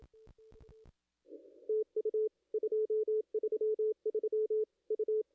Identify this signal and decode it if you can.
Morse code, heard as a keyed tone, about 27 wpm, 430 Hz. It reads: TU 233 U
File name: contests / ss-cw-k5tr-op-n5rz -0200